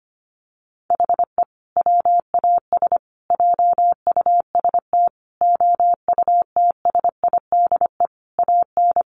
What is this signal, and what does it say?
Morse code, heard as a keyed tone, about 25 wpm, 700 Hz. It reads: HE WAS JUST OUTSIDE AN
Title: Sentences from Top 1100 Words 25wpm